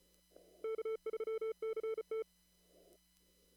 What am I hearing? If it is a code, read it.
Morse code, heard as a keyed tone, about 34 wpm, 440 Hz. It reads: K3CT
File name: CW / kt5j wpx cw-op-k5pi